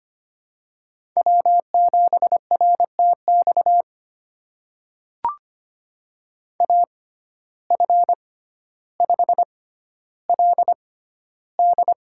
Morse code, heard as a keyed tone, about 25 wpm, spaced as for 13 wpm, 700 Hz. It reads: W7RTX AF5LD
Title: USA Call Signs - 5 Characters - 6x Character Spacing - 25wpm